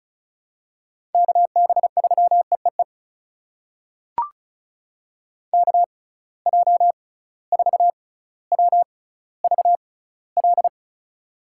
Morse code, heard as a keyed tone, about 34 wpm, spaced as for 18 wpm, 700 Hz. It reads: KB3EEE KJ4WVL
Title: USA Call Signs - 6x Character Spacing - 35wpm